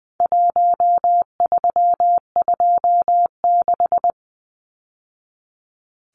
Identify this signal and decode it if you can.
Morse code, heard as a keyed tone, about 20 wpm, 700 Hz. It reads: 1326
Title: Numbers - 4-Digit 20wpm